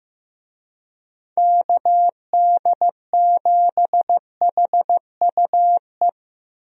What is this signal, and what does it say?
Morse code, heard as a keyed tone, about 15 wpm, 700 Hz. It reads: KD7HUE
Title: USA Call Signs - 6 Characters - Repeated 3x - 15wpm